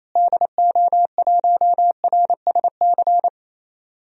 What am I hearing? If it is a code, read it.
Morse code, heard as a keyed tone, about 28 wpm, 700 Hz. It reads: DO1RSC
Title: DXCC-100 Call Signs - 6 Characters - Repeated 2x - 28wpm